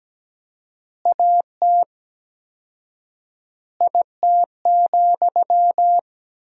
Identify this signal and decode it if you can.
Morse code, heard as a keyed tone, about 17 wpm, 700 Hz. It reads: AT IT,
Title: 4x Word-Spacing - Sentences from Top 200 Words - 17wpm